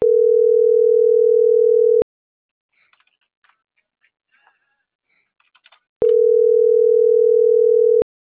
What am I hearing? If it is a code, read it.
Ringback tone (North America), 440 + 480 Hz.